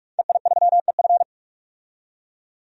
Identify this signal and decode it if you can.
Morse code, heard as a keyed tone, about 45 wpm, 700 Hz. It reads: EI3EF
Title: DXCC-100 Call Signs - 5 Characters - 45wpm